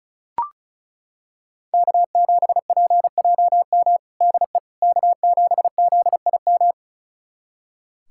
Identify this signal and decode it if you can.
Morse code, heard as a keyed tone, about 36 wpm, 700 Hz. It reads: K7PJM DE K7ZIM
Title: USA Callsign DE Callsign - 5 Characters - 35wpm